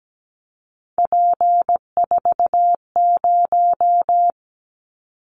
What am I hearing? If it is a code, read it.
Morse code, heard as a keyed tone, about 17 wpm, 700 Hz. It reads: P40
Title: Callsign Prefixes for Top 100 DXCC - 17wpm